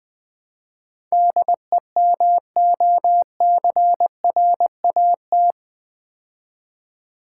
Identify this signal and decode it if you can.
Morse code, heard as a keyed tone, about 20 wpm, 700 Hz. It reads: DEMOCRAT
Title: Build A Word - 4 Letters At A Time - 10-Letter Words - 20wpm